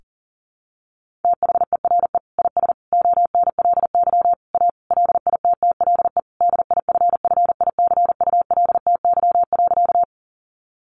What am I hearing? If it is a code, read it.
Morse code, heard as a keyed tone, about 40 wpm, 700 Hz. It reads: THERE IS ONLY A LITTLE DIFFICULTY.
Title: Sentences from Top 1500 Words 40wpm